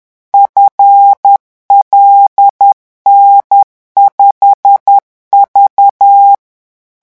Morse code, heard as a keyed tone, about 10 wpm, 800 Hz. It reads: FLN5V
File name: morse mp3 / revision13-vitesse 10 mots